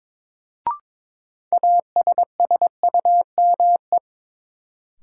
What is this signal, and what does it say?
Morse code, heard as a keyed tone, about 22 wpm, 700 Hz. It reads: ASSUME
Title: Sentences from Top 1500 Words - New Word Drill - 22wpm